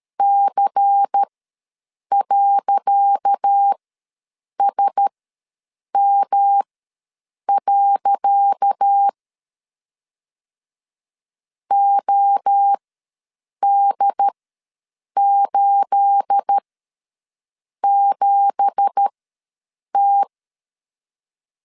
Morse code, heard as a keyed tone, about 13 wpm, spaced as for 7 wpm, 800 Hz. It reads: C.SM. OD87T
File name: LEZIONI / 10 SIGNS / 10C DOT ws 12wpm 3548